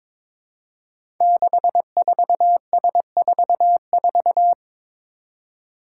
Morse code, heard as a keyed tone, about 22 wpm, 700 Hz. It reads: 64S44
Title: Five Challenging Letter-Number Combinations 22wpm